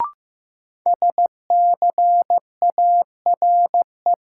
Morse code, heard as a keyed tone, about 15 wpm, 700 Hz. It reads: SCARE
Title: Sentences from Top 1000 Words - Review Part 5 - New Word Drill - 15wpm